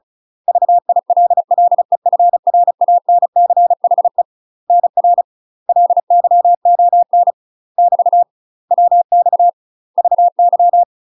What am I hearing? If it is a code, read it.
Morse code, heard as a keyed tone, about 34 wpm, 700 Hz. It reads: VILLEFRANCHE NR LYON <BT> WX VY
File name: UK MASTER QSOs TRAINING / 35WPM / UK MASTER QSOs 0034